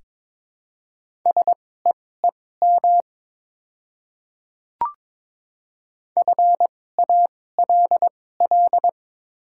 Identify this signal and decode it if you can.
Morse code, heard as a keyed tone, about 23 wpm, spaced as for 16 wpm, 700 Hz. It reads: SEEM FALL
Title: 2x Character Spacing - Top 500 Words - 22wpm